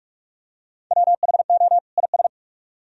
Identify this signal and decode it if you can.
Morse code, heard as a keyed tone, about 44 wpm, 700 Hz. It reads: WHO IS